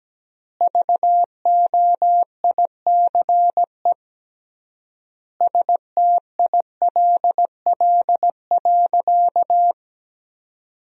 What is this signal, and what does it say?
Morse code, heard as a keyed tone, about 17 wpm, 700 Hz. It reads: VOICE STILL.
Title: 3x Word-Spacing - Sentences from Top 500 Words - 17wpm